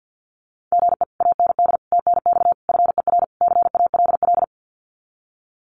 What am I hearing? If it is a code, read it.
Morse code, heard as a keyed tone, about 48 wpm, 700 Hz. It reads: GE AND TNX FER CALL
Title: Common QSO Phrases Encore 50wpm